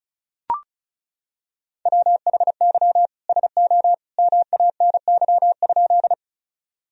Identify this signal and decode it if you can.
Morse code, heard as a keyed tone, about 36 wpm, 700 Hz. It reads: WHY SO MANY?
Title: Sentences from Top 1000 Words - Review Part 1 - 35wpm